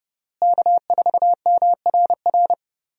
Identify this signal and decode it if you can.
Morse code, heard as a keyed tone, about 30 wpm, 700 Hz. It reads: K4MRR
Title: USA Call Signs - 5 Characters - Repeated 3x - 30wpm